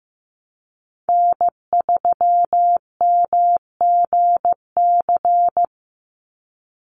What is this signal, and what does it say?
Morse code, heard as a keyed tone, about 15 wpm, 700 Hz. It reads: N3MGC
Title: USA Call Signs 15wpm